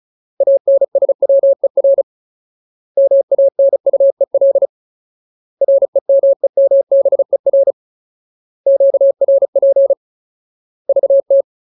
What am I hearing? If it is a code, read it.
Morse code, heard as a keyed tone, about 34 wpm, 550 Hz. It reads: ANSWER MANUEL REMEMBER QRP VT
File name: Ultimate Word List / 35WPM Ultimate